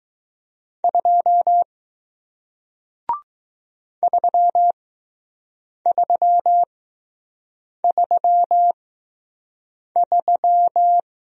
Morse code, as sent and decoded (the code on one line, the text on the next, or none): ..--- / ...-- ...-- ...-- ...--
2 3333